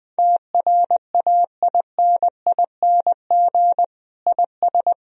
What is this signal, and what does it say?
Morse code, heard as a keyed tone, about 20 wpm, 700 Hz. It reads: TRAINING IS
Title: Sentences from Top 1000 Words - Review Part 4 - 20wpm